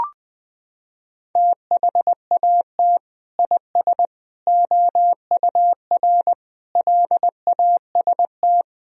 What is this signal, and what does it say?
Morse code, heard as a keyed tone, about 20 wpm, 700 Hz. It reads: THAT IS OUR LAST
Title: Sentences from Top 1000 Words - Review Part 4 - 20wpm